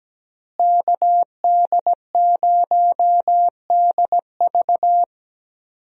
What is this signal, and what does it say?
Morse code, heard as a keyed tone, about 17 wpm, 700 Hz. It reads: KD0DV
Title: USA Call Signs - 5 Characters - Repeated 3x - 17wpm